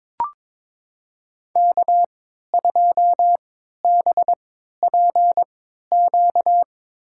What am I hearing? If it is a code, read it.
Morse code, heard as a keyed tone, about 21 wpm, spaced as for 12 wpm, 700 Hz. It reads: K2BPQ
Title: USA Call Signs - 5 Characters - 3x Character Spacing - 22wpm